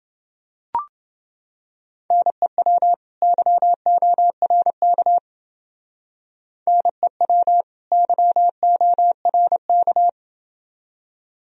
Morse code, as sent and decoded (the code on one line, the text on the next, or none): -. .-- / -.-- --- .-. -.- / -. .-- / -.-- --- .-. -.-
NEW YORK NEW YORK